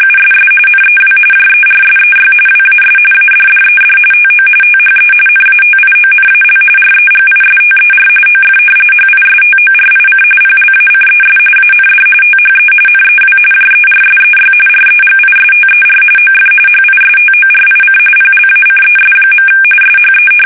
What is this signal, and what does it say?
RTTY, about 45.45 baud. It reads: USS NEW JERSEY WAS ADDED TO THE NATIONAL REGISTER OF HISTORIC PLACES AND THE NEW JERSEY REGISTER OF HISTORIC PLACES. SHE
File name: fsk850 / battleshipnj